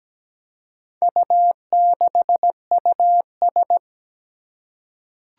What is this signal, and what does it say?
Morse code, heard as a keyed tone, about 17 wpm, 700 Hz. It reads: U6US
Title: Four Challenging Letter-Number Combinations 17wpm